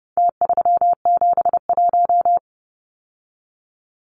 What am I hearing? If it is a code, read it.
Morse code, heard as a keyed tone, about 30 wpm, 700 Hz. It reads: T371
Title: Numbers - Contest Style - 4-Digit 30wpm